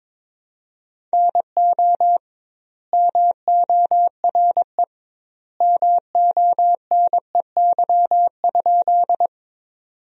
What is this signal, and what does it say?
Morse code, heard as a keyed tone, about 21 wpm, 700 Hz. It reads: NO MORE MONEY?
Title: 2x Word-Spacing - Sentences from Top 400 Words - 22wpm